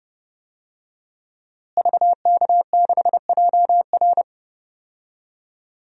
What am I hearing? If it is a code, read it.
Morse code, heard as a keyed tone, about 30 wpm, 700 Hz. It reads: VK6JR
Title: DXCC-100 Call Signs - 5 Characters - Repeated 2x - 30wpm